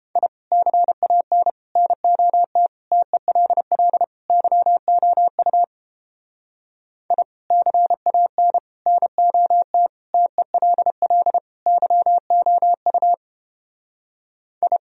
Morse code, heard as a keyed tone, about 29 wpm, 700 Hz. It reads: I CAN NOT TELL YOU I CAN NOT TELL YOU I